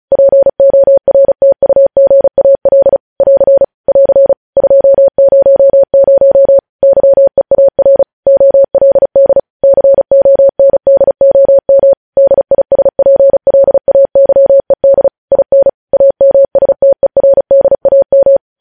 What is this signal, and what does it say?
Morse code, heard as a keyed tone, about 36 wpm, 550 Hz. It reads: PORTUGAL <AR> <AR> 200 YEAR OLD CONDOM DISPLAYED IN AMSTERDAM
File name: podcast / News headlines-1x-35x35